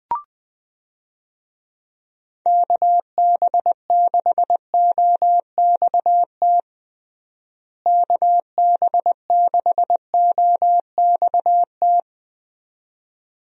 Morse code, heard as a keyed tone, about 20 wpm, 700 Hz. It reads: KB6OXT KB6OXT
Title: USA Call Signs - 6 Characters - Repeated 2x - 20wpm